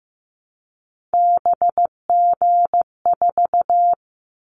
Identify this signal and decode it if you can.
Morse code, heard as a keyed tone, about 15 wpm, 700 Hz. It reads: BG4
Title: Callsign Prefixes for Top 100 DXCC - 15wpm